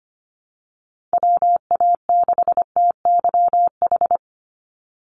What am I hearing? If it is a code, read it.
Morse code, heard as a keyed tone, about 25 wpm, 700 Hz. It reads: WA6TYH
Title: USA Call Signs 25wpm